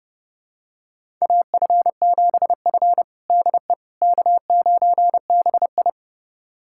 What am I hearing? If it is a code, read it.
Morse code, heard as a keyed tone, about 32 wpm, 700 Hz. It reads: AF7F DE K9BI
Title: USA Callsign DE Callsign - 4 Characters - 30wpm